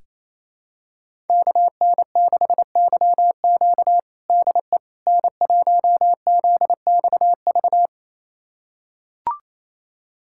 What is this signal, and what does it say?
Morse code, heard as a keyed tone, about 27 wpm, 700 Hz. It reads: KN6YQ DE N1ZXV
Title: USA Callsign DE Callsign - 5 Characters - 28wpm